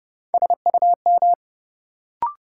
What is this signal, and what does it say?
Morse code, heard as a keyed tone, about 30 wpm, 700 Hz. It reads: SUM
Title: Sentences from Top 1000 Words - Review Part 6 - New Word Drill - 30wpm